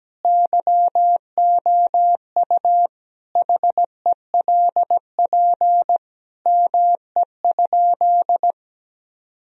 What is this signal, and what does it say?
Morse code, heard as a keyed tone, about 17 wpm, 700 Hz. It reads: YOU HELP ME?
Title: Sentences from Top 1000 Words - Review Part 3 - 17wpm